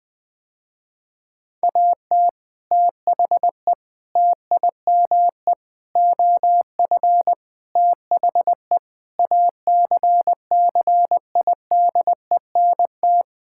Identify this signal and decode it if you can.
Morse code, heard as a keyed tone, about 20 wpm, 700 Hz. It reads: AT THE TIME OF THE ACCIDENT
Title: Sentences with 5-Word Phrases - Spoken Beginning - 20wpm